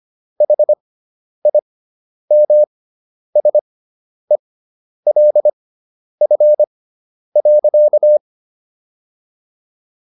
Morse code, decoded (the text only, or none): HIMSELF.